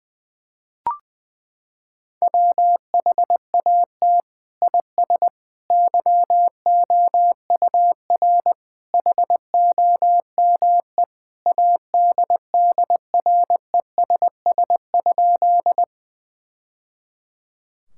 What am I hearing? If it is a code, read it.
Morse code, heard as a keyed tone, about 20 wpm, 700 Hz. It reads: WHAT IS YOUR HOME ADDRESS?
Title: Sentences from Top 1000 Words - Review Part 6 - 20wpm